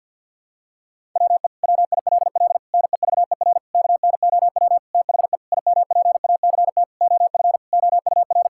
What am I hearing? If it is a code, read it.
Morse code, heard as a keyed tone, about 52 wpm, 700 Hz. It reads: WE WILL NEVER KNOW THE IMPACT OF OUR
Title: Sentences from Top 1000 Words - Review Part 6 - 50wpm